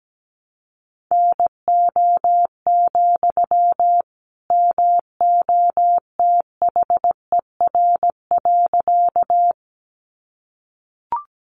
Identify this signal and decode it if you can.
Morse code, heard as a keyed tone, about 17 wpm, 700 Hz. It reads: NO, MOTHER.
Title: Sentences from Top 200 Words 17wpm